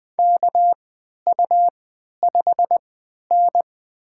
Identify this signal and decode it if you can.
Morse code, heard as a keyed tone, about 20 wpm, spaced as for 11 wpm, 700 Hz. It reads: KU5N
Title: USA Call Signs - 4 Characters - 3x Character Spacing - 20wpm